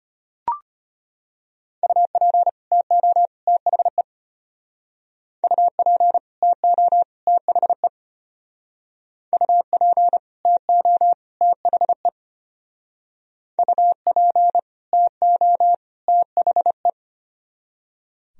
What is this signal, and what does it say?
Morse code, heard as a keyed tone, about 33 wpm, 700 Hz. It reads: UP TO THE UP TO THE UP TO THE UP TO THE